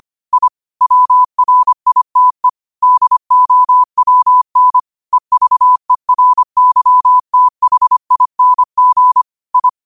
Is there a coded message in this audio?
Yes: Morse code from 0.3 to 9.7 s, heard as a keyed tone, about 25 wpm, 1000 Hz. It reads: I WRITE DOWN EVERYTHING I